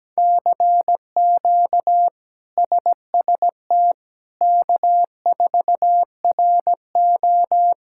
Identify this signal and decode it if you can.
Morse code, heard as a keyed tone, about 17 wpm, 700 Hz. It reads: CQ SST K4RO
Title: K1USN SST QSO Practice - 17wpm